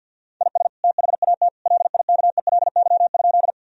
Morse code, heard as a keyed tone, about 50 wpm, 700 Hz. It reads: IS THAT LIKELY?
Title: Sentences from Top 1000 Words - Review Part 4 - 50wpm